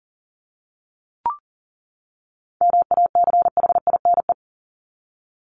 Morse code, heard as a keyed tone, about 40 wpm, 700 Hz. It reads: MACHINE
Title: Top 500 Words 40wpm